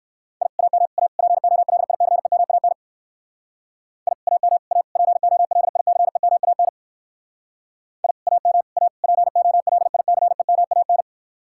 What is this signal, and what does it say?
Morse code, heard as a keyed tone, about 59 wpm, 700 Hz. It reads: I AM A POLICEMAN I AM A POLICEMAN I AM A POLICEMAN